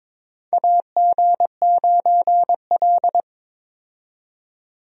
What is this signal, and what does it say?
Morse code, heard as a keyed tone, about 21 wpm, 700 Hz. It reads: AG9L
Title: USA Call Signs - 4 Characters - 22wpm